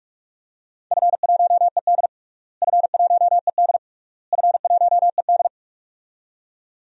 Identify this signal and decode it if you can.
Morse code, heard as a keyed tone, about 46 wpm, 690 Hz. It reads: F1ED F1ED F1ED
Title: DXCC-100 Call Signs - 4 Characters - Repeated 3x - 45wpm